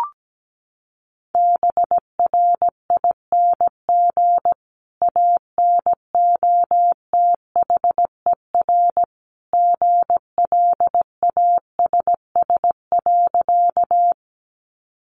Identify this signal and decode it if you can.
Morse code, heard as a keyed tone, about 17 wpm, 700 Hz. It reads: BRING ANOTHER GLASS.
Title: Sentences from Top 1300 Words 17wpm